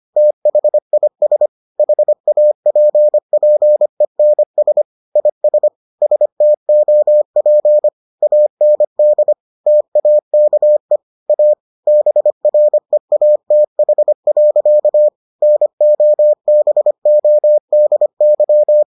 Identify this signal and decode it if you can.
Morse code, heard as a keyed tone, about 25 wpm, 600 Hz. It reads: THIS HAPPENS IS STOP AND TAKE A BREATH. NOBODY